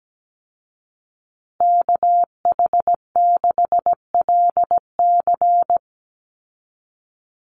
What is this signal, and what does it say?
Morse code, heard as a keyed tone, about 17 wpm, 700 Hz. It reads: KH6LC
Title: Call Signs Repeated 3x - 17wpm